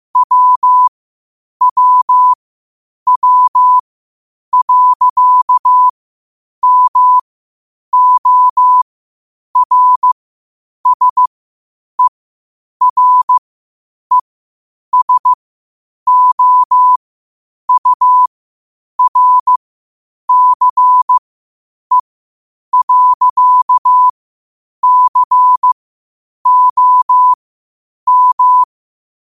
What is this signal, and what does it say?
Morse code, heard as a keyed tone, about 15 wpm, spaced as for 8 wpm, 1000 Hz. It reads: WWW.MORSERESOURCE.COM
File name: qotdfiles / 8wpm